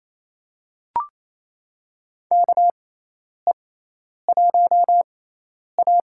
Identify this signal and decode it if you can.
Morse code, heard as a keyed tone, about 28 wpm, spaced as for 10 wpm, 700 Hz. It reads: KE1A
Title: USA Call Signs - 4 Characters - 6x Character Spacing - 28wpm